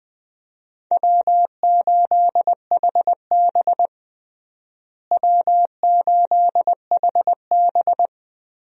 Morse code, heard as a keyed tone, about 21 wpm, 700 Hz. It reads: W8HB W8HB
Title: USA Call Signs - 4 Characters - Repeated 3x - 20wpm